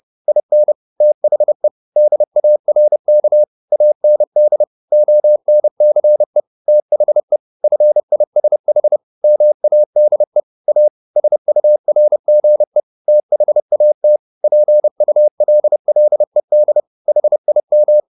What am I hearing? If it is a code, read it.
Morse code, heard as a keyed tone, about 30 wpm, 600 Hz. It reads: IN THE DARK AND ONCE THE FISH MADE A SURGE THAT PULLED HIM
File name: the old man and the sea / the old man and the sea 1 30wpm